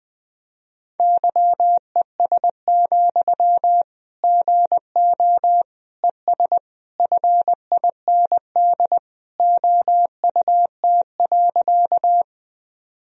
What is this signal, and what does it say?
Morse code, heard as a keyed tone, about 20 wpm, 700 Hz. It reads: YES, GO ES FIND OUT.
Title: Sentences from Top 500 Words and Abbreviations 20wpm